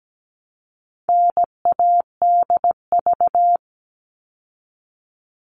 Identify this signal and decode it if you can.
Morse code, heard as a keyed tone, about 17 wpm, 700 Hz. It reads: NADV
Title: All Cut Numbers - 4-Digit 17wpm